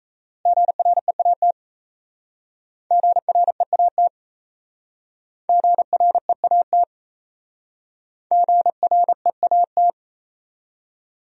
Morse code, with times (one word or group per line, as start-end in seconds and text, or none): 0.5-1.5 GREAT
2.9-4.1 GREAT
5.5-6.8 GREAT
8.3-9.9 GREAT